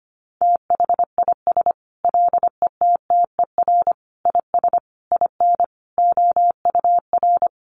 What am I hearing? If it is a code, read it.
Morse code, heard as a keyed tone, about 25 wpm, 700 Hz. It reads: THIS LETTER IS IN OUR
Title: Sentences from Top 1400 Words 25wpm